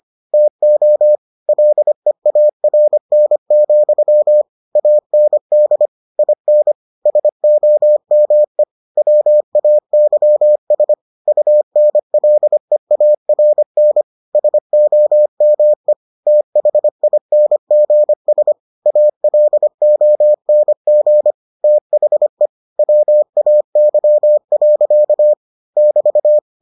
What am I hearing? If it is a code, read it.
Morse code, heard as a keyed tone, about 25 wpm, 600 Hz. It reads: TO LEARN, AND IN SOME WAYS UNLEARN SOME THINGS ALONG THE WAY. <BT>